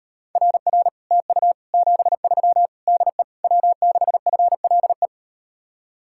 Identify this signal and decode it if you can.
Morse code, heard as a keyed tone, about 38 wpm, 700 Hz. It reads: RR TU 73 DE W6FLE